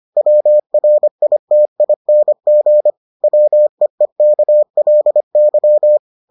Morse code, heard as a keyed tone, about 25 wpm, 600 Hz. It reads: WRITING WEEKLY